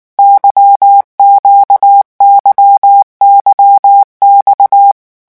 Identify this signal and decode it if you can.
Morse code, heard as a keyed tone, about 19 wpm, 800 Hz. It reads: YQYYX
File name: revision1-vitesse 18 mots